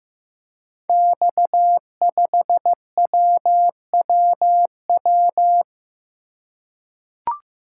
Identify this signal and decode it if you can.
Morse code, heard as a keyed tone, about 15 wpm, 700 Hz. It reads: X5WWW
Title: Code Groups - 5 Letter:Numbers - 15wpm